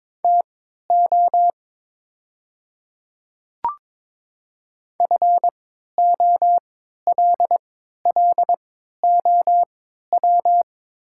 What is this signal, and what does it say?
Morse code, heard as a keyed tone, about 22 wpm, spaced as for 13 wpm, 700 Hz. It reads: TO FOLLOW
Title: 3x Character Spacing - Top 300 Words - 22wpm